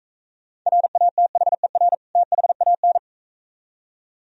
Morse code, heard as a keyed tone, about 42 wpm, 700 Hz. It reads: RATHER THAN